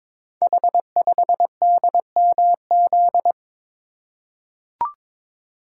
Morse code, heard as a keyed tone, about 22 wpm, 700 Hz. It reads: H5DMZ
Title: Code Groups - 5 Letter:Numbers - 22wpm